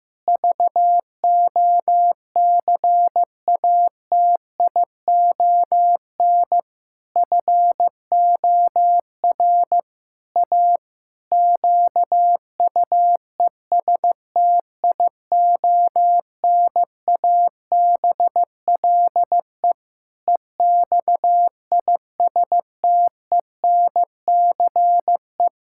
Morse code, heard as a keyed tone, about 14 wpm, 700 Hz. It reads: VOCATION FOR A QUESTIONABLE EXISTENCE
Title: Pangram Challenge v2 - 15wpm